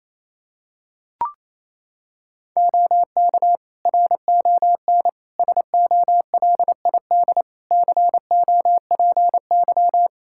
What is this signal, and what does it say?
Morse code, heard as a keyed tone, about 28 wpm, 700 Hz. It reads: OK RON SOLID COPY
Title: Practice QSO - Exchange 3 - More Info - 28wpm